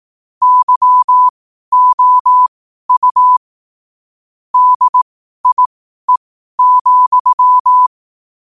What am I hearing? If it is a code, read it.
Morse code, heard as a keyed tone, about 18 wpm, spaced as for 12 wpm, 1000 Hz. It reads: YOU DIE,
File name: qotdfiles / 12wpm